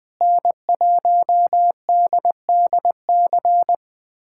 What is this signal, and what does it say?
Morse code, heard as a keyed tone, about 20 wpm, 700 Hz. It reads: N1DDC
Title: USA Callsign DE Callsign - 5 Characters - 20wpm